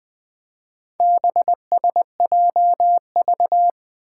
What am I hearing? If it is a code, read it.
Morse code, heard as a keyed tone, about 20 wpm, 700 Hz. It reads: BSJV
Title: Four Challenging Letter-Number Combinations 20wpm